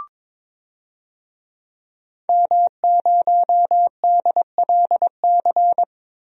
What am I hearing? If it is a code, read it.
Morse code, heard as a keyed tone, about 22 wpm, 700 Hz. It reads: M0DLC
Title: DXCC-100 Call Signs - 5 Characters - 22wpm